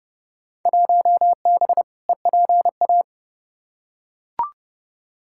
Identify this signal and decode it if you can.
Morse code, heard as a keyed tone, about 30 wpm, 700 Hz. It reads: 1B EPA
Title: Field Day Exchanges Encore - 30wpm